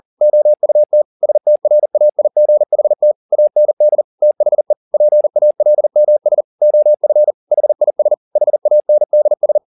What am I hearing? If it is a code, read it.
Morse code, heard as a keyed tone, about 40 wpm, 600 Hz. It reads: OUT STRAIGHT AND THE PALMS OF HIS HANDS
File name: the old man and the sea 2 40wpm